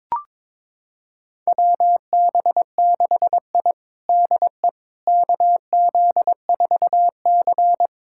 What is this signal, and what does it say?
Morse code, heard as a keyed tone, about 22 wpm, 700 Hz. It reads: WB6I DE KZ4C
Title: USA Callsign DE Callsign - 4 Characters - 22wpm